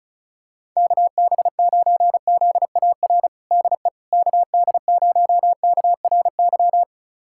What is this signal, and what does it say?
Morse code, heard as a keyed tone, about 34 wpm, 700 Hz. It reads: KB9ZAR DE KD0KRY
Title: USA Callsign DE Callsign - 6 Characters - 35wpm